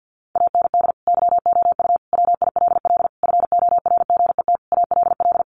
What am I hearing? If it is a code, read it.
Morse code, heard as a keyed tone, about 50 wpm, 700 Hz. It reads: AND YOU WILL FORGET ALL
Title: Sentences from Top 1200 Words 50wpm